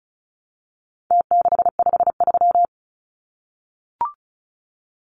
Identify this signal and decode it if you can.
Morse code, heard as a keyed tone, about 34 wpm, 700 Hz. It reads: T653
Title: Numbers - Contest Style - 4-Digit 35wpm